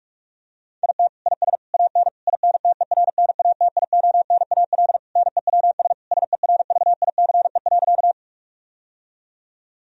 Morse code, heard as a keyed tone, about 46 wpm, 700 Hz. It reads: IT IS AN INTERNATIONAL NEWS SERVICE.
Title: Sentences from Top 1000 Words - Review Part 4 - 45wpm